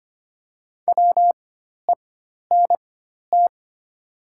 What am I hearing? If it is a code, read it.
Morse code, heard as a keyed tone, about 25 wpm, spaced as for 12 wpm, 700 Hz. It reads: WENT
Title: 4x Character Spacing - Top 200 Words - 25wpm